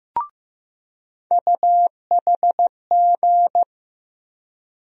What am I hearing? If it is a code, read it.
Morse code, heard as a keyed tone, about 15 wpm, 700 Hz. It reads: UHG
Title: USA Call Sign Postfixes - 3 Letters 15wpm